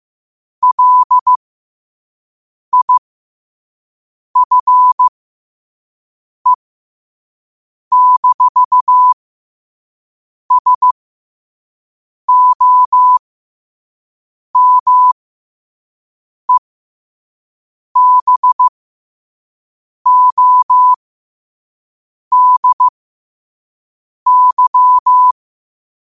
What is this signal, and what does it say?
Morse code, heard as a keyed tone, about 15 wpm, spaced as for 5 wpm, 1000 Hz. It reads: LIFE-SOMEBODY
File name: qotdfiles / 5wpm